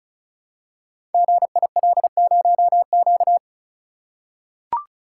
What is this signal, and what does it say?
Morse code, heard as a keyed tone, about 36 wpm, 700 Hz. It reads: GIL0Q
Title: Code Groups - 5 Letter:Numbers - 35wpm